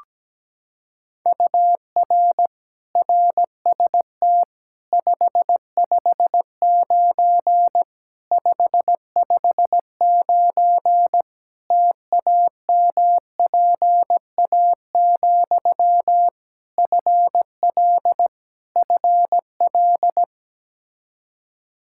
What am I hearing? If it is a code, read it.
Morse code, heard as a keyed tone, about 17 wpm, 700 Hz. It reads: UR RST 559 559 TAMPA, FL FL
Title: Practice QSO - Exchange 1 - Beginning - 17wpm